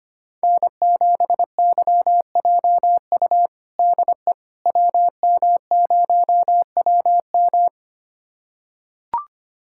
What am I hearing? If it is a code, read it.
Morse code, heard as a keyed tone, about 25 wpm, 700 Hz. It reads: N7YJU DE WM0WM
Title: USA Callsign DE Callsign - 5 Characters - 25wpm